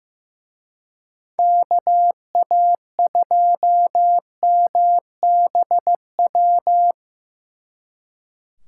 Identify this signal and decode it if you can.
Morse code, heard as a keyed tone, about 17 wpm, 700 Hz. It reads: KA2MBW
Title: Speed Racing - USA 2x3 Call Signs 15wpm